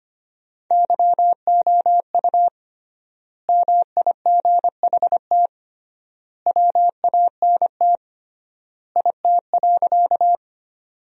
Morse code, heard as a keyed tone, about 25 wpm, 700 Hz. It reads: YOU MIGHT WANT IT.
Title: 3x Word-Spacing - Sentences from Top 300 Words - 25wpm